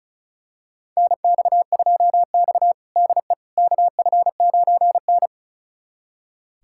Morse code, heard as a keyed tone, about 36 wpm, 700 Hz. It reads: NX2X DE KF9N
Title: USA Callsign DE Callsign - 4 Characters - 35wpm